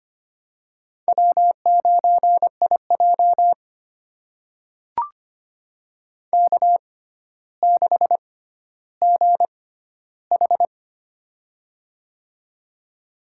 Morse code, heard as a keyed tone, about 25 wpm, spaced as for 13 wpm, 700 Hz. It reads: W9IJ K6GH
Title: USA Call Signs - 6x Character Spacing - 25wpm